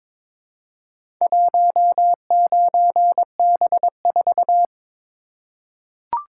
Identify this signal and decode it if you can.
Morse code, heard as a keyed tone, about 22 wpm, 700 Hz. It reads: 19B4
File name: practice / 22wpm / Four Letter-Number Combinations - 22wpm